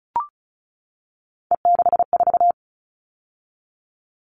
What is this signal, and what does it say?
Morse code, heard as a keyed tone, about 34 wpm, 700 Hz. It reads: E64
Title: All Cut Numbers - 3-Digit 35wpm